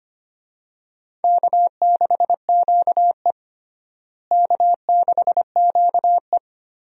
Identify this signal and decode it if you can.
Morse code, heard as a keyed tone, about 25 wpm, 700 Hz. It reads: K6QE K6QE
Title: USA Call Signs - 4 Characters - Repeated 3x - 25wpm